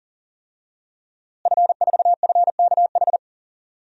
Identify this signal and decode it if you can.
Morse code, heard as a keyed tone, about 37 wpm, 700 Hz. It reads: F4FKH
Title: DXCC-100 Call Signs - 5 Characters - Repeated 3x - 40wpm